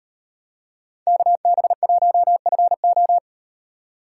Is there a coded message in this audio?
Yes: Morse code from 1.1 to 3.2 s, heard as a keyed tone, about 36 wpm, 700 Hz. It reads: KB1FO